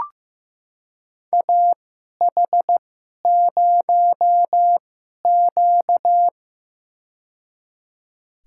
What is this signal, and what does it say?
Morse code, heard as a keyed tone, about 15 wpm, spaced as for 11 wpm, 700 Hz. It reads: AH0Q
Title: USA Call Signs - 4 Characters - 2x Character Spacing - 15wpm